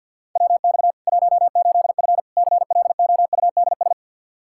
Morse code, heard as a keyed tone, about 48 wpm, 700 Hz. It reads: WX 18F CLOUDS
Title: Practice QSO - Exchange 2 - Info - 50wpm